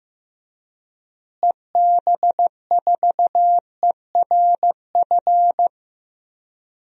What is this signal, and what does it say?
Morse code, heard as a keyed tone, about 15 wpm, 700 Hz. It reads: EB4ERF
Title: DXCC-100 Call Signs - 6 Characters - Repeated 2x - 15wpm